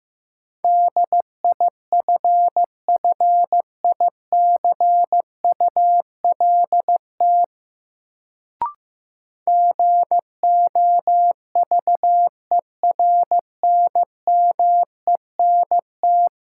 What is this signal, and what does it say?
Morse code, heard as a keyed tone, about 15 wpm, 700 Hz. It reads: DIFFICULT GOVERNMENT
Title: Sentences from Top 600 Words - New Word Drill - 15wpm